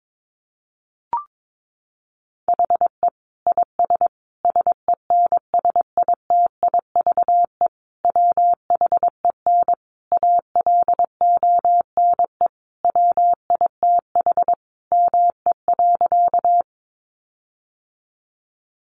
Morse code, heard as a keyed tone, about 22 wpm, 700 Hz. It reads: HE IS SENSITIVE WHEN ALONE WITH ME.
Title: Sentences from Top 1400 Words 22wpm